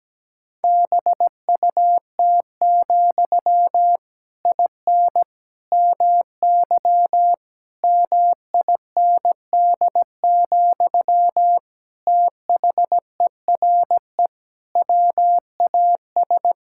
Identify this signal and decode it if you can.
Morse code, heard as a keyed tone, about 17 wpm, 700 Hz. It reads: BUT, IN MY MIND, THERE WAS
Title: Sentences from Top 1000 Words - Review Part 5 - 17wpm